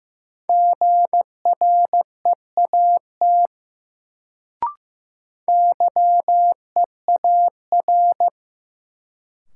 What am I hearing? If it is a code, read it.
Morse code, heard as a keyed tone, about 15 wpm, 700 Hz. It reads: GREAT YEAR
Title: Sentences from Top 200 Words - New Word Drill - 15wpm